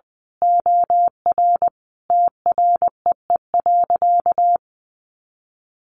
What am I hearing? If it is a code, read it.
Morse code, heard as a keyed tone, about 20 wpm, 700 Hz. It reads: OR TREE.
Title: A Sound Of Thunder 20wpm